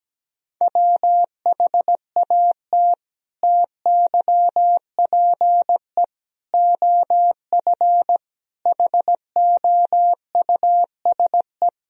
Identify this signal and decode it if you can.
Morse code, heard as a keyed tone, about 17 wpm, 700 Hz. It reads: WHAT TYPE OF HOUSE
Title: Sentences from Top 1000 Words - Review Part 3 - 17wpm